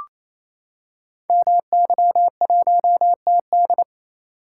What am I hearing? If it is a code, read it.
Morse code, heard as a keyed tone, about 27 wpm, 700 Hz. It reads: MY1TD
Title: Code Groups - 5 Letter:Numbers - 28wpm